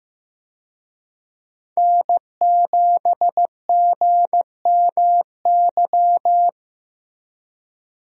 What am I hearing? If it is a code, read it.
Morse code, heard as a keyed tone, about 15 wpm, 700 Hz. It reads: N7GMY
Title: USA Call Signs - 5 Characters - Repeated 3x - 15wpm